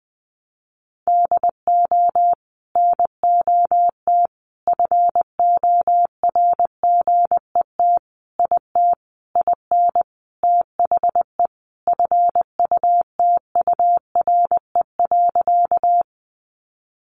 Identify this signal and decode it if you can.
Morse code, heard as a keyed tone, about 20 wpm, 700 Hz. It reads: DO NOT FORGET IT IN THE FUTURE.
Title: Sentences from Top 1200 Words 20wpm